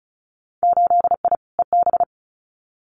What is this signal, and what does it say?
Morse code, heard as a keyed tone, about 36 wpm, 700 Hz. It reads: 8I EB
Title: Winter Field Day Exchanges Encore 35wpm